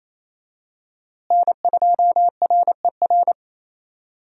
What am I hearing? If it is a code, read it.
Morse code, heard as a keyed tone, about 28 wpm, 700 Hz. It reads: N2RER
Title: USA Call Signs - 5 Characters - Repeated 2x - 28wpm